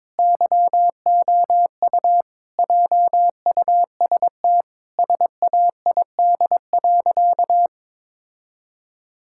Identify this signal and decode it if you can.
Morse code, heard as a keyed tone, about 21 wpm, 700 Hz. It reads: YOU JUST SAID.
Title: Sentences from Top 1600 Words 22wpm